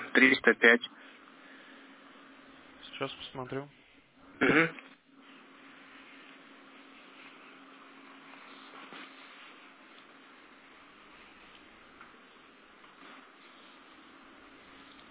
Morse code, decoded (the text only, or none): none